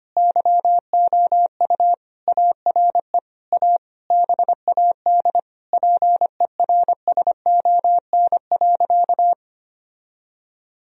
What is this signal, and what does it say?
Morse code, heard as a keyed tone, about 25 wpm, 700 Hz. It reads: YOU ARE A BAD PERSON.
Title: Sentences from Top 1000 Words - Review Part 3 - 25wpm